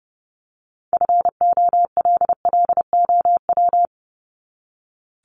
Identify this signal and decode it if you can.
Morse code, heard as a keyed tone, about 30 wpm, 700 Hz. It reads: FOLLOW
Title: ING Suffix - Top 100 Words with Base Words 30wpm